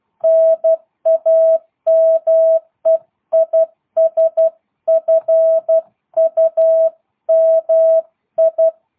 Morse code, heard as a keyed tone, about 12 wpm, 650 Hz. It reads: NAME IS FUMI
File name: jh8cbh / 2022cw